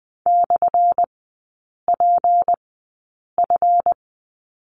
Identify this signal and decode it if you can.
Morse code, heard as a keyed tone, about 20 wpm, spaced as for 8 wpm, 700 Hz. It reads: /PF
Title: 178 Lesson 29e - Slash - Three Characters 20wpm